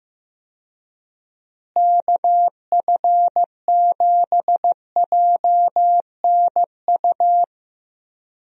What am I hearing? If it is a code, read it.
Morse code, heard as a keyed tone, about 15 wpm, 700 Hz. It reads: KF7JNU
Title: USA Call Signs - 6 Characters - Repeated 2x - 15wpm